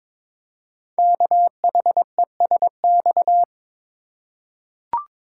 Morse code, heard as a keyed tone, about 23 wpm, 700 Hz. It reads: KHESX
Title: Code Groups - 5 Letters - 22wpm